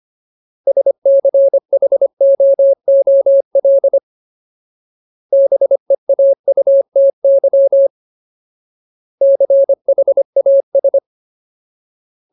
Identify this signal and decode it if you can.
Morse code, heard as a keyed tone, about 25 wpm, 550 Hz. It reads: SCHOOL BEAUTY CHAS
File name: Ultimate Word List / 25WPM Ultimate